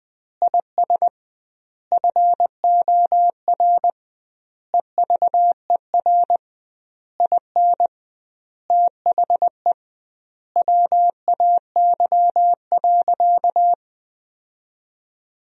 Morse code, heard as a keyed tone, about 20 wpm, 700 Hz. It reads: IS FOR EVER IN THE WAY.
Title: 2x Word-Spacing - Sentences from Top 400 Words - 20wpm